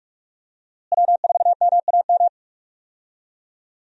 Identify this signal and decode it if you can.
Morse code, heard as a keyed tone, about 43 wpm, 700 Hz. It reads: W4MAM
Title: USA Call Signs - 5 Characters - Repeated 2x - 45wpm